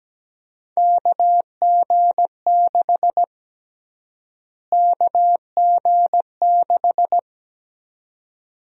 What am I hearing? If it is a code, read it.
Morse code, heard as a keyed tone, about 17 wpm, 700 Hz. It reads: KG6 KG6
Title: USA Call Sign Prefixes - Repeated 2x - 17wpm